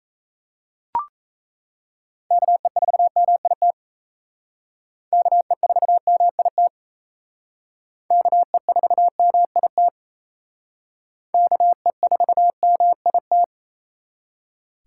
Morse code, heard as a keyed tone, about 35 wpm, 700 Hz. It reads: KE4MIT KE4MIT KE4MIT KE4MIT